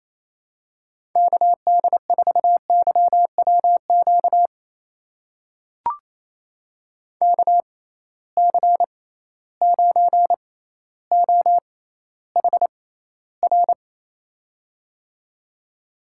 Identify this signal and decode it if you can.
Morse code, heard as a keyed tone, about 28 wpm, spaced as for 14 wpm, 700 Hz. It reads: KD4YWQ KC9OHR
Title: USA Call Signs - 6 Characters - 6x Character Spacing - 28wpm